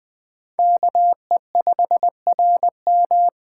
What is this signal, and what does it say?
Morse code, heard as a keyed tone, about 20 wpm, 700 Hz. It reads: KE5RM
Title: Speed Racing - USA 2x2 Call Signs 17wpm